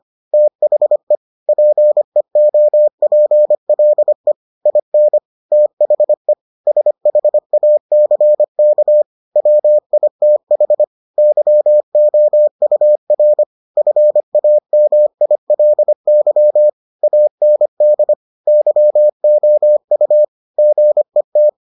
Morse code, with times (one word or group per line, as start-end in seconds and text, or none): 0.3-1.1 THE
1.5-4.3 PEOPLE
4.7-5.2 IN
5.5-6.3 THE
6.7-9.0 SHACK
9.4-10.8 WITH
11.2-13.4 YOUR
13.8-16.7 FAMILY
17.0-18.1 AND
18.5-20.2 YOU
20.6-21.5 GET